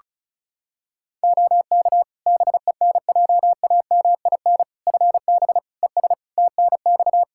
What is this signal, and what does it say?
Morse code, heard as a keyed tone, about 34 wpm, 700 Hz. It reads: OK BENJAMIN FB ES TNX
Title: K1USN SST QSO Practice 35wpm